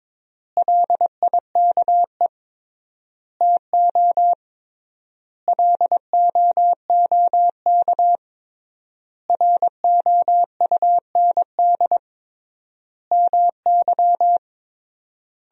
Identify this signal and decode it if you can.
Morse code, heard as a keyed tone, about 22 wpm, 700 Hz. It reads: LIKE TO LOOK ROUND MY
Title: 3x Word-Spacing - Sentences from Top 200 Words - 22wpm